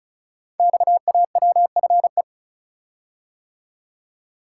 Morse code, heard as a keyed tone, about 36 wpm, 700 Hz. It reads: XAWFE
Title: Code Groups - 5 Letters - 35wpm